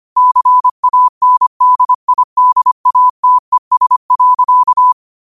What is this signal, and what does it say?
Morse code, heard as a keyed tone, about 25 wpm, 1000 Hz. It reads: CANDIDATES.
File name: qotdfiles / 25wpm